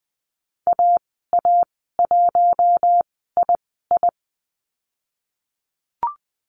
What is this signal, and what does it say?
Morse code, heard as a keyed tone, about 20 wpm, spaced as for 15 wpm, 700 Hz. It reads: AA1II
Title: 048 Lesson 9g - Number 4 - Call Signs 20wpm